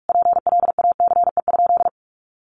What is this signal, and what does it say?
Morse code, heard as a keyed tone, about 44 wpm, 700 Hz. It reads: PLACE?
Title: Sentences from Top 200 Words 45wpm